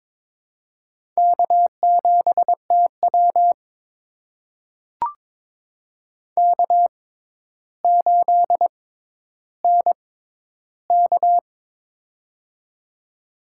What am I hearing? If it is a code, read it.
Morse code, heard as a keyed tone, about 22 wpm, spaced as for 11 wpm, 700 Hz. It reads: K7TW K8NK
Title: USA Call Signs - 4 Characters - 6x Character Spacing - 22wpm